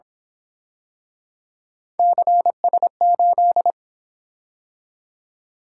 Morse code, heard as a keyed tone, about 26 wpm, 700 Hz. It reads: CS8
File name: Speed-Racing - Three Letter-Number Combinations - 22wpm